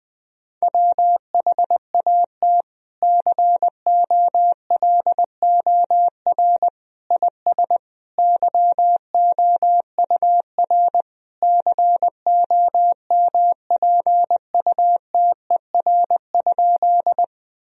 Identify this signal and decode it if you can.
Morse code, heard as a keyed tone, about 21 wpm, 700 Hz. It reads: WHAT COLOR IS YOUR COMPUTER?
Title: Sentences from Top 1000 Words - Review Part 3 - 20wpm